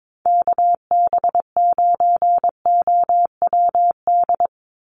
Morse code, heard as a keyed tone, about 22 wpm, 700 Hz. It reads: KB9OWD
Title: Call Signs - 22wpm